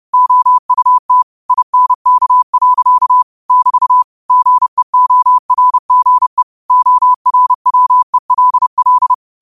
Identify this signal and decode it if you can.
Morse code, heard as a keyed tone, about 30 wpm, 1000 Hz. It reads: OUT INK. <BT> GEORGE ORWELL
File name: qotdfiles / 30wpm